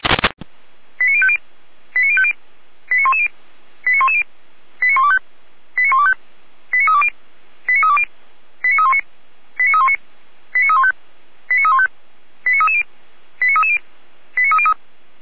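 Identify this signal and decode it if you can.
Five-tone selective call (ZVEI), heard as quick runs of five tones, tones of about 80 ms. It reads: 89050 89050 89110 89110 89125 89125 89320 89320 89319 89319 89315 89315 89330 89330 89303
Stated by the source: ZVEI five-tone selective call 89042, received on SDR.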